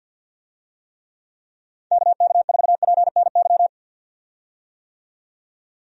Morse code, heard as a keyed tone, about 46 wpm, 700 Hz. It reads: KK4PNY
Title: USA Call Signs - 6 Characters - 50wpm